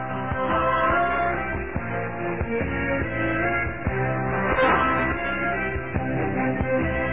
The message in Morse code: none